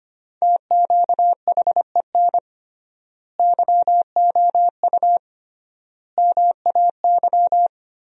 Morse code, heard as a keyed tone, about 25 wpm, 700 Hz. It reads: TQHEN YOU MAY
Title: 3x Word-Spacing - Sentences from Top 100 Words - 25wpm